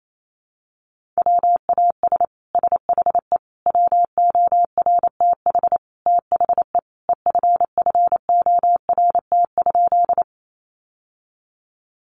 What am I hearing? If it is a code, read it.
Morse code, heard as a keyed tone, about 28 wpm, 700 Hz. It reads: WAS SHE WORTH THE EFFORT?
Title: Sentences from Top 1200 Words 28wpm